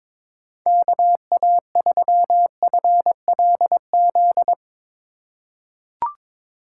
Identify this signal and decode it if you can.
Morse code, heard as a keyed tone, about 22 wpm, 700 Hz. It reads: KA3FLZ
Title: USA Call Signs - 6 Characters - 3x Character Spacing - 22wpm